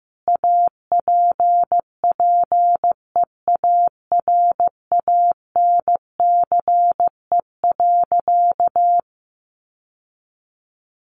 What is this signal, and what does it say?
Morse code, heard as a keyed tone, about 15 wpm, 700 Hz. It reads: APPEARANCE.
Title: Sentences from Top 1100 Words 15wpm